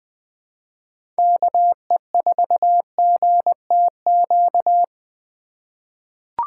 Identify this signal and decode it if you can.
Morse code, heard as a keyed tone, about 20 wpm, 700 Hz. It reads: KE4GTQ
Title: 2023 POTA Activator Call Signs - 20wpm